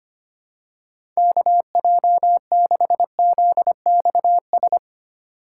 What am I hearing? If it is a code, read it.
Morse code, heard as a keyed tone, about 25 wpm, 700 Hz. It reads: KJ6ZXS
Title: USA Call Signs - 3x Character Spacing - 25wpm